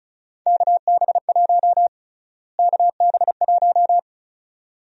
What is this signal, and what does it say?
Morse code, heard as a keyed tone, about 36 wpm, 700 Hz. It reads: KB1 KB1
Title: USA Call Sign Prefixes - Repeated 2x - 35wpm